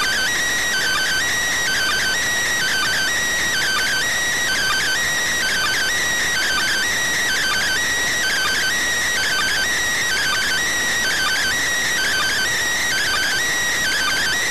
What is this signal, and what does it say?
Two-tone teleprinter signal, about 400 Hz shift.